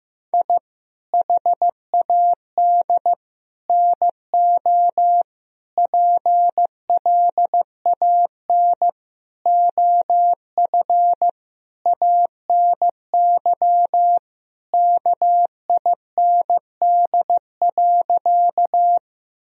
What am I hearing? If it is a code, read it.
Morse code, heard as a keyed tone, about 15 wpm, 700 Hz. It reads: I HAD NO PLAN OF ANY KIND.
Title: Sentences from Top 1000 Words - Review Part 2 - 15wpm